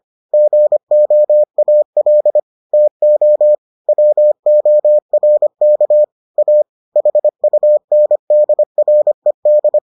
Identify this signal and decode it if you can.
Morse code, heard as a keyed tone, about 25 wpm, 600 Hz. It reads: GOAL TO WORK A HUNDRED